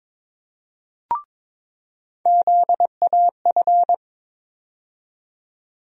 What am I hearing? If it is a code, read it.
Morse code, heard as a keyed tone, about 22 wpm, 700 Hz. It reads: ZAF
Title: USA Call Sign Postfixes - 3 Letters 22wpm